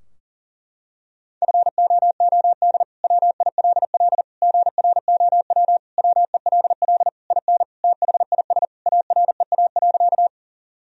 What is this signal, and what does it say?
Morse code, heard as a keyed tone, about 40 wpm, 700 Hz. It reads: FOOD WILL GROW WELL IN THIS AREA.
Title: Sentences from Top 1000 Words - Review Part 2 - 40wpm